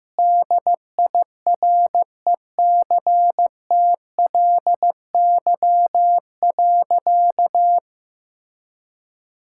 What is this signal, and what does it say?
Morse code, heard as a keyed tone, about 15 wpm, 700 Hz. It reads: DIRECTLY.
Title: Sentences from Top 1000 Words - Review Part 5 - 15wpm